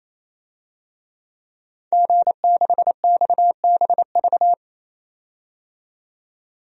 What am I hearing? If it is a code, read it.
Morse code, heard as a keyed tone, about 28 wpm, 700 Hz. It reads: G6XBV
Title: DXCC-100 Call Signs - 5 Characters - 28wpm